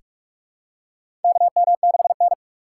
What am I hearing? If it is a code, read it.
Morse code, heard as a keyed tone, about 46 wpm, 700 Hz. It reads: KM6N
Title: USA Call Signs - 4x Character Spacing - 45wpm